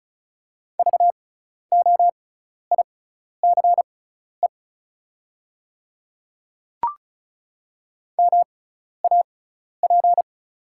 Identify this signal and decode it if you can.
Morse code, heard as a keyed tone, about 36 wpm, spaced as for 12 wpm, 690 Hz. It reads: VOICE MAP
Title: 6x Character Spacing - Top 500 Words - 35wpm